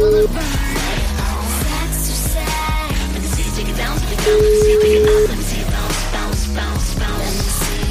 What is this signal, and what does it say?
Ringback tone (Europe, 425 Hz), with noise.